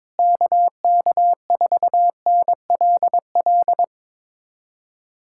Morse code, heard as a keyed tone, about 22 wpm, 700 Hz. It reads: KK4NLL
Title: USA Callsign DE Callsign - 6 Characters - 22wpm